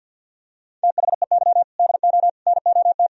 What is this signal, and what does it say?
Morse code, heard as a keyed tone, about 48 wpm, 700 Hz. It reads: THEY DO NOT
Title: Sentences from Top 1000 Words - Review Part 1 - 50wpm